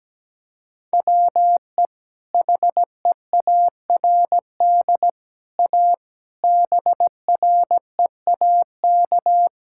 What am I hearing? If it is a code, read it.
Morse code, heard as a keyed tone, about 17 wpm, 700 Hz. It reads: WE HEARD A BREAK
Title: Sentences from Top 1000 Words - Review Part 6 - 17wpm